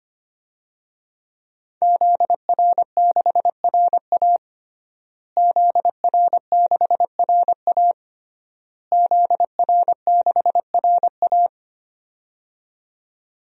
Morse code, heard as a keyed tone, about 25 wpm, 700 Hz. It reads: ZR6RA ZR6RA ZR6RA
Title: DXCC-100 Call Signs - 5 Characters - Repeated 3x - 25wpm